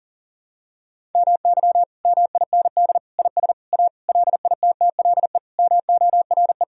Morse code, heard as a keyed tone, about 40 wpm, 700 Hz. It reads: MY MIND IS A LITTLE MORE
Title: Sentences from Top 1000 Words - Review Part 1 - 40wpm